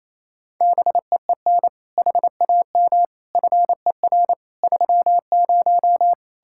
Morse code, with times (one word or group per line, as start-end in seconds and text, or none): 0.6-1.7 BEEN
2.0-3.0 HAM
3.4-4.3 FER
4.6-6.1 30